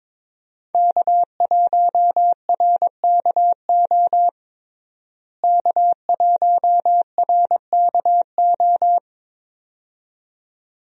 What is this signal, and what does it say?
Morse code, heard as a keyed tone, about 22 wpm, 700 Hz. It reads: K1RKO K1RKO
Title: USA Call Signs - 5 Characters - Repeated 2x - 22wpm